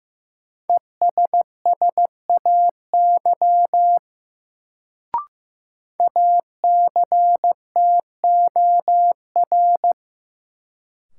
Morse code, heard as a keyed tone, about 15 wpm, 700 Hz. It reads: ESSAY ACTOR
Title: Sentences from Top 1500 Words - New Word Drill - 15wpm